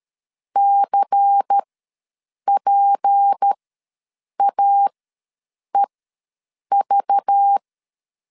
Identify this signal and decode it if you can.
Morse code, heard as a keyed tone, about 13 wpm, spaced as for 7 wpm, 800 Hz. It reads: CPAEV
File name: LEZIONI / 6 PKX0 / 6 P ws 12wpm 3117